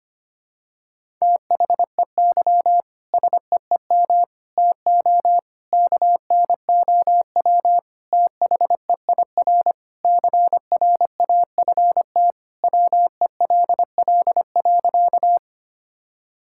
Morse code, heard as a keyed tone, about 25 wpm, 700 Hz. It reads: THEY SEEM TO KNOW THEIR CRAFT WELL.
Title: Sentences from Top 1000 Words - Review Part 6 - 25wpm